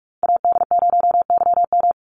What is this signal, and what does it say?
Morse code, heard as a keyed tone, about 44 wpm, 700 Hz. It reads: AD0YM
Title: NA SOTA Chaser Callsigns 45wpm